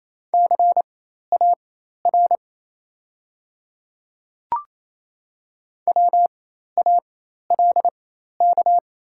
Morse code, heard as a keyed tone, about 27 wpm, spaced as for 13 wpm, 700 Hz. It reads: CAR WALK
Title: 4x Character Spacing - Top 300 Words - 28wpm